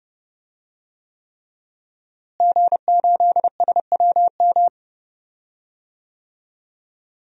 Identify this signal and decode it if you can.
Morse code, heard as a keyed tone, about 30 wpm, 700 Hz. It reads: G8SWM
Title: DXCC-100 Call Signs - 5 Characters - Repeated 3x - 30wpm